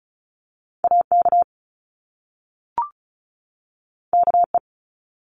Morse code, heard as a keyed tone, about 36 wpm, 700 Hz. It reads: AK KE
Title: Top Two Letter Combinations in Words 35wpm